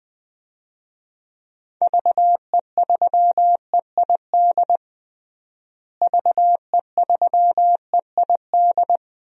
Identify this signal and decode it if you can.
Morse code, heard as a keyed tone, about 21 wpm, 700 Hz. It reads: VE3EID VE3EID
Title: DXCC-100 Call Signs - 6 Characters - Repeated 3x - 20wpm